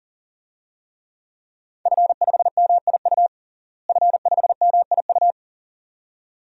Morse code, heard as a keyed tone, about 40 wpm, 700 Hz. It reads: F5MIU F5MIU
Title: DXCC-100 Call Signs - 5 Characters - Repeated 2x - 40wpm